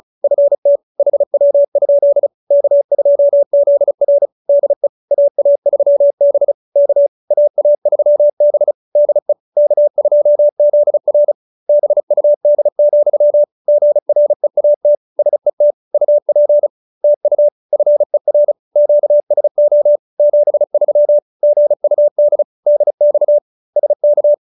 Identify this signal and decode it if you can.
Morse code, heard as a keyed tone, about 34 wpm, 590 Hz. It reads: FT HW? K2ZR DE AA3B K AA3B DE K2ZR BUD, GREAT SET UP TU FER QSO 73 GUD DX SK
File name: QSO302 35